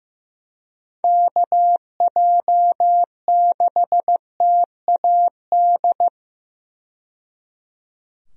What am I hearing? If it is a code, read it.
Morse code, heard as a keyed tone, about 15 wpm, 700 Hz. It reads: KJ6TAD
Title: USA Call Signs - 6 Characters - 15wpm